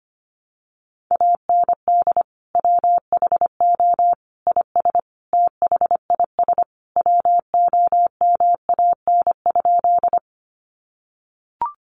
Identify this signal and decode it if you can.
Morse code, heard as a keyed tone, about 25 wpm, 700 Hz. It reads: AND WHO IS THIS WOMAN?
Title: Sentences from Top 1100 Words 25wpm